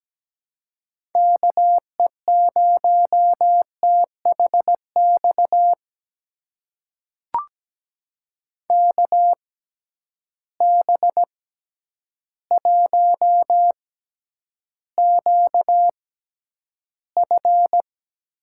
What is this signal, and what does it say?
Morse code, heard as a keyed tone, about 17 wpm, spaced as for 9 wpm, 700 Hz. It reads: KE0THX KB1QF
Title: USA Call Signs - 6 Characters - 6x Character Spacing - 17wpm